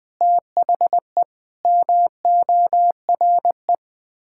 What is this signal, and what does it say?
Morse code, heard as a keyed tone, about 20 wpm, 700 Hz. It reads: THE MORE
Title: Sets of 8 Words - 20wpm